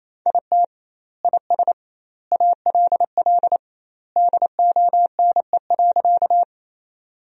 Morse code, heard as a keyed tone, about 28 wpm, 700 Hz. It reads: IT IS ALL DONE.
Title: 2x Word-Spacing - Sentences from Top 500 Words - 28wpm